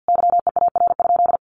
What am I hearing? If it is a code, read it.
Morse code, heard as a keyed tone, about 50 wpm, 700 Hz. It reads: YEAR?
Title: Sentences from Top 1500 Words 50wpm